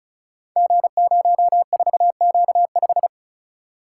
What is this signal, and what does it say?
Morse code, heard as a keyed tone, about 34 wpm, 700 Hz. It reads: G04Q5
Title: Code Groups - 5 Letter:Numbers - 35wpm